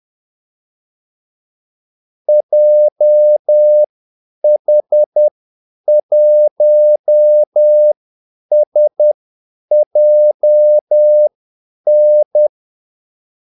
Morse code, heard as a keyed tone, about 10 wpm, spaced as for 8 wpm, 600 Hz. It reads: JH1SJN
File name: Callsigns-10-8wpm